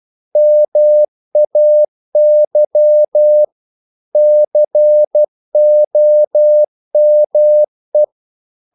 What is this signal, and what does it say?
Morse code, heard as a keyed tone, about 12 wpm, 600 Hz. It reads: MAY COME
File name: CW / Practice / EnglishQuotations0312